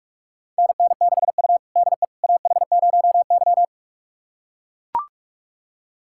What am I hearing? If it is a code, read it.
Morse code, heard as a keyed tone, about 44 wpm, 690 Hz. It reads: NN6U DE AH0Y
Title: USA Callsign DE Callsign - 4 Characters - 45wpm